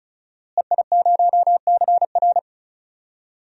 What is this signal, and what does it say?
Morse code, heard as a keyed tone, about 36 wpm, 700 Hz. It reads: EI0CR
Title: DXCC-100 Call Signs - 5 Characters - Repeated 2x - 35wpm